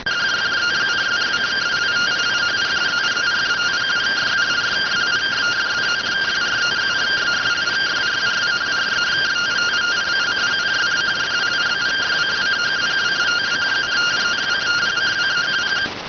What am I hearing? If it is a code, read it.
Two-tone teleprinter signal, about 170 Hz shift.